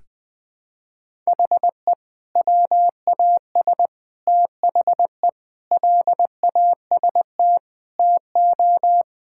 Morse code, heard as a keyed tone, about 20 wpm, 700 Hz. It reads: HE WAS THE LAST TO
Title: Sentences with 2-Word Phrases - 20wpm